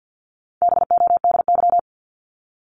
Morse code, heard as a keyed tone, about 49 wpm, 700 Hz. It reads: BODY
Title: Top 500 Words 50wpm